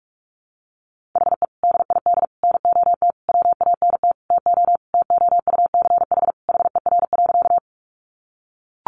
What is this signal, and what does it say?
Morse code, heard as a keyed tone, about 44 wpm, 700 Hz. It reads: HE DID NOT WANT TO TOUCH HER.
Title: Sentences from Top 1200 Words 45wpm